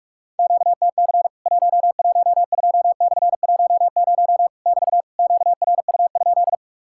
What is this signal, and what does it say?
Morse code, heard as a keyed tone, about 46 wpm, 700 Hz. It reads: QTC 112/10 <BT> QRU?
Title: Worked All Europe DX Contest - QTC Exchange - 45wpm